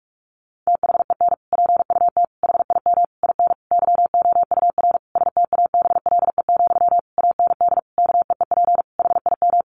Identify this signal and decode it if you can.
Morse code, heard as a keyed tone, about 44 wpm, 700 Hz. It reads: THEN PUT HIM IN YOUR STABLE, AND KEEP HIM
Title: Sentences from Top 1500 Words 45wpm